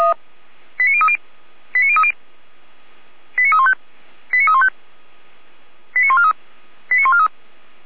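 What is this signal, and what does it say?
Five-tone selective call (ZVEI), heard as quick runs of five tones, tones of about 70 ms. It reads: 89030 89030 89315 89315 89143 89143
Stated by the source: ZVEI five-tone call 89305 received with an SDR, associated with feuerwehr-orlamuende (Orlamünde fire brigade).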